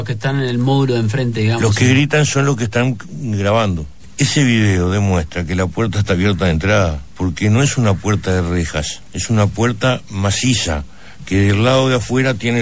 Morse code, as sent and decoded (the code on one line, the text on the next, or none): none